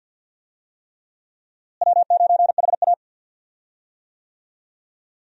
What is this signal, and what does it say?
Morse code, heard as a keyed tone, about 47 wpm, 700 Hz. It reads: W9HA
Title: USA Call Signs - 4 Characters - Repeated 3x - 50wpm